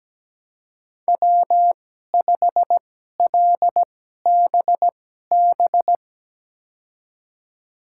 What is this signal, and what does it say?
Morse code, heard as a keyed tone, about 17 wpm, spaced as for 12 wpm, 700 Hz. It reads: W5LBB
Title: USA Call Signs - 5 Characters - 2x Character Spacing - 17wpm